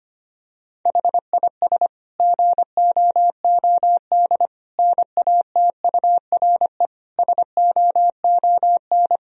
Morse code, heard as a keyed tone, about 25 wpm, 700 Hz. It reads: HIS GOOD NATURE SOON
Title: Sentences from Top 1000 Words - Review Part 1 - 25wpm